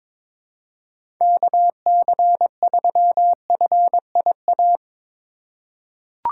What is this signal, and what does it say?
Morse code, heard as a keyed tone, about 22 wpm, 700 Hz. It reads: KC3FIA
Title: USA Call Signs - 6 Characters - 4x Character Spacing - 22wpm